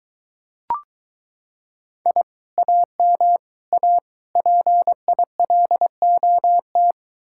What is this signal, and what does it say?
Morse code, heard as a keyed tone, about 22 wpm, 700 Hz. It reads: I AM A PILOT